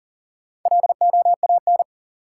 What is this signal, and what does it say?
Morse code, heard as a keyed tone, about 40 wpm, 700 Hz. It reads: LOAN
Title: Four Letter Words - 40wpm